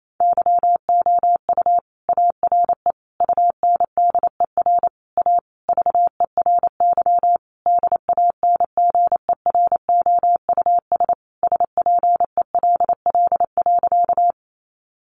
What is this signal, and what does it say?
Morse code, heard as a keyed tone, about 28 wpm, 700 Hz. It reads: YOU ARE UNDER A VERY DANGEROUS SPELL.
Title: Sentences from Top 1200 Words 28wpm